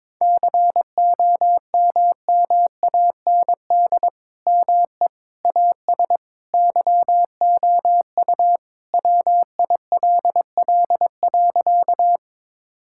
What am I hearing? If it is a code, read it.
Morse code, heard as a keyed tone, about 22 wpm, 700 Hz. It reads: COMMAND ME AS YOU WILL.
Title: Sentences from Top 1600 Words 22wpm